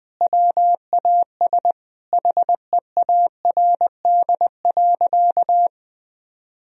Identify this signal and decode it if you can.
Morse code, heard as a keyed tone, about 20 wpm, 700 Hz. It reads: WAS HEARD.
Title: Sentences from Top 400 Words and Abbreviations 20wpm